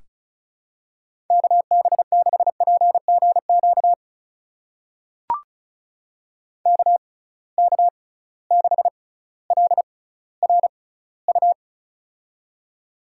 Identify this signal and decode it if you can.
Morse code, heard as a keyed tone, about 34 wpm, spaced as for 18 wpm, 700 Hz. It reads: KB6PGQ KK6LRU
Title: USA Call Signs - 6 Characters - 6x Character Spacing - 35wpm